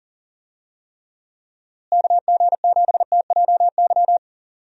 Morse code, heard as a keyed tone, about 37 wpm, 700 Hz. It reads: KG7TJY
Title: USA Call Signs - 6 Characters - Repeated 3x - 40wpm